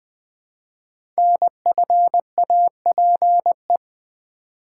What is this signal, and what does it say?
Morse code, heard as a keyed tone, about 20 wpm, 700 Hz. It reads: NFAPE
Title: Code Groups - 5 Letters - 20wpm